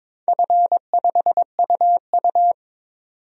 Morse code, heard as a keyed tone, about 21 wpm, 700 Hz. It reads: F5UU
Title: DXCC-100 Call Signs - 4 Characters - 22wpm